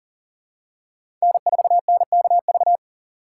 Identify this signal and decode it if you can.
Morse code, heard as a keyed tone, about 37 wpm, 700 Hz. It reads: N4NKV
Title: USA Call Signs - 4x Character Spacing - 40wpm